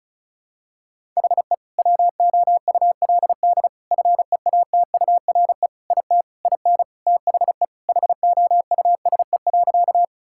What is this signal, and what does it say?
Morse code, heard as a keyed tone, about 34 wpm, 700 Hz. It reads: HE WOULD FEATURE IT IN THE HOUSE.
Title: Sentences from Top 1000 Words - Review Part 5 - 35wpm